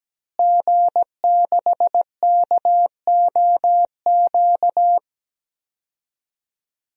Morse code, heard as a keyed tone, about 17 wpm, 700 Hz. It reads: G6KOQ
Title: DXCC-100 Call Signs - 5 Characters - Repeated 2x - 17wpm